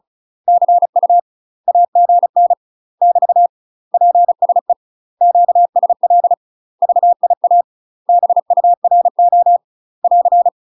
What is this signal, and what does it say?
Morse code, heard as a keyed tone, about 36 wpm, 700 Hz. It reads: CU AGN <BT> PSE QSL VIA BURO <AR>
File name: UK MASTER QSOs TRAINING / 35WPM / UK MASTER QSOs 0028